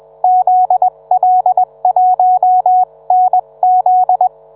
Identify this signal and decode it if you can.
Morse code, heard as a keyed tone, about 21 wpm, 740 Hz. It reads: ZL1NZ